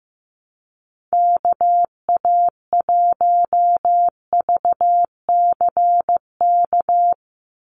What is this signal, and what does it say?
Morse code, heard as a keyed tone, about 15 wpm, 700 Hz. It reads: KA1VCK
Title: USA Call Signs 15wpm